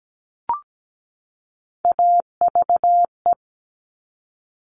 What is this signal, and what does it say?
Morse code, heard as a keyed tone, about 17 wpm, 700 Hz. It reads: AVE
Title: Top Three Letter Combinations in Words 17wpm